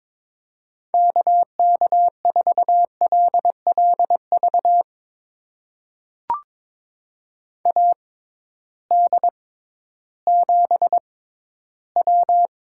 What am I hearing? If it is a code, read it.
Morse code, heard as a keyed tone, about 22 wpm, spaced as for 13 wpm, 700 Hz. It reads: KK4LLV AD7W
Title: USA Call Signs - 6x Character Spacing - 22wpm